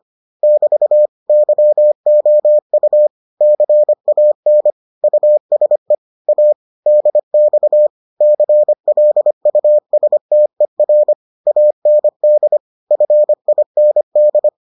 Morse code, heard as a keyed tone, about 25 wpm, 600 Hz. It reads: <BT> YOU CAN USE A DX CLUSTER AND FIND